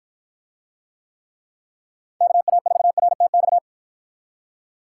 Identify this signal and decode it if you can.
Morse code, heard as a keyed tone, about 50 wpm, 700 Hz. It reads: KA4RTX